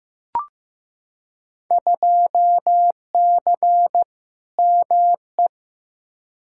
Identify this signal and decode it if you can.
Morse code, heard as a keyed tone, about 15 wpm, 700 Hz. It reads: 2C ME
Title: Field Day Exchanges Encore - 15wpm